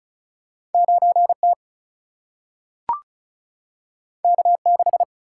Morse code, heard as a keyed tone, about 36 wpm, 700 Hz. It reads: 9 T K 6
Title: Single-Letter Single-Number 35wpm